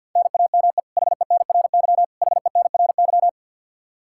Morse code, heard as a keyed tone, about 52 wpm, 700 Hz. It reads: NAME HENRY HENRY
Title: Practice QSO - Exchange 1 - Beginning - 50wpm